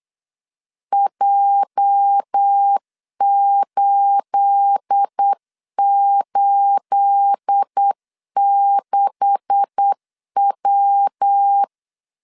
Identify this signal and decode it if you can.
Morse code, heard as a keyed tone, about 8 wpm, 800 Hz. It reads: J886W
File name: CW / DET / 1 9 1500C S A 8wpm 5637